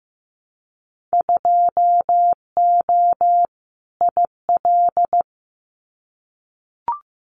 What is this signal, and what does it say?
Morse code, heard as a keyed tone, about 15 wpm, 700 Hz. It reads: 2O IL
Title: Winter Field Day Exchanges 15wpm